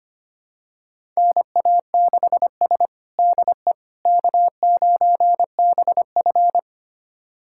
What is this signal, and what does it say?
Morse code, heard as a keyed tone, about 25 wpm, 700 Hz. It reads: NA6S DE K9BF
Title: USA Callsign DE Callsign - 4 Characters - 25wpm